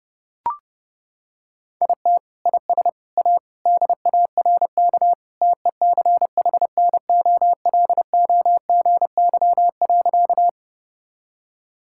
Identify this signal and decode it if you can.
Morse code, heard as a keyed tone, about 30 wpm, 700 Hz. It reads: IT IS A DARK TECHNOLOGY.
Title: Sentences from Top 1000 Words - Review Part 5 - 30wpm